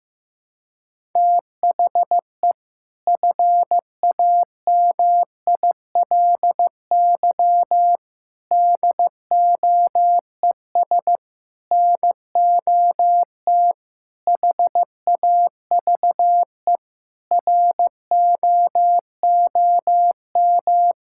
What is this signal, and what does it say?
Morse code, heard as a keyed tone, about 15 wpm, 700 Hz. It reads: THE FAMILY DOES NOT HAVE ROOM
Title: Sentences from Top 1000 Words - Review Part 5 - 15wpm